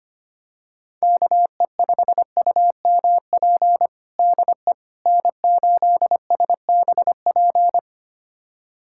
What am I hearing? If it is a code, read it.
Morse code, heard as a keyed tone, about 25 wpm, 700 Hz. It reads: KE5UMP DE N8SBP
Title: USA Callsign DE Callsign 25wpm